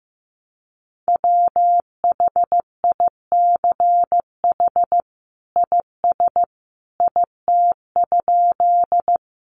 Morse code, heard as a keyed tone, about 14 wpm, 700 Hz. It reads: WHICH IS IT?
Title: Sentences from Top 100 Words 15wpm